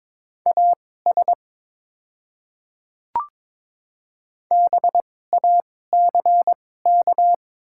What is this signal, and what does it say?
Morse code, heard as a keyed tone, about 22 wpm, spaced as for 16 wpm, 700 Hz. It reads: AS BACK
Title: 2x Character Spacing - Top 300 Words - 22wpm